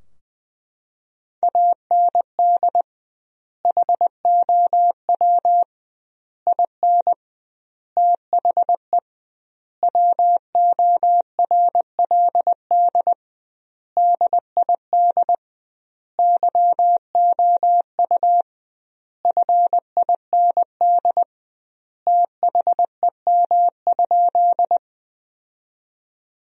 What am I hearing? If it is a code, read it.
Morse code, heard as a keyed tone, about 20 wpm, 700 Hz. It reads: AND HOW IN THE WORLD DID YOU FIND THEM?
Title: 2x Word-Spacing - Sentences from Top 200 Words - 20wpm